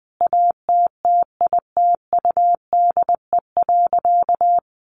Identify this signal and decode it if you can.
Morse code, heard as a keyed tone, about 20 wpm, 700 Hz. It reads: ATTITUDE.
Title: Sentences from Top 1300 Words 20wpm